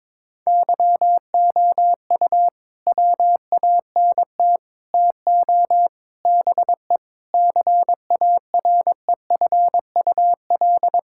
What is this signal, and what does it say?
Morse code, heard as a keyed tone, about 22 wpm, 700 Hz. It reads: YOU WANT TO BE CAREFUL
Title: Sentences from Top 1600 Words 22wpm